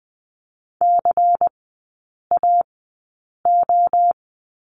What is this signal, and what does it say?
Morse code, heard as a keyed tone, about 20 wpm, spaced as for 8 wpm, 700 Hz. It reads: CAO
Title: 127 Lesson 21e - Number 6 - Three Characters 20wpm